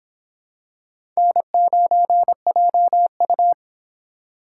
Morse code, heard as a keyed tone, about 24 wpm, 700 Hz. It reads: N9JU